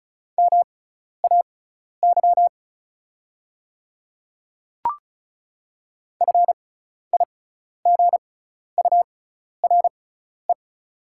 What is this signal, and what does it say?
Morse code, heard as a keyed tone, about 36 wpm, spaced as for 12 wpm, 700 Hz. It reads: MAY FIGURE
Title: 6x Character Spacing - Top 500 Words - 35wpm